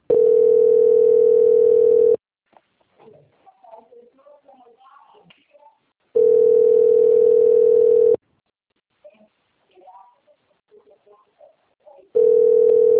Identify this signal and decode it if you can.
Ringback tone (North America), 440 + 480 Hz, with speech.
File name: rg-1600-unknown- - - .18812